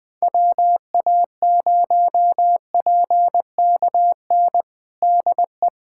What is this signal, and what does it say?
Morse code, heard as a keyed tone, about 20 wpm, 700 Hz. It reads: WA0PKN DE
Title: USA Callsign DE Callsign - 6 Characters - 20wpm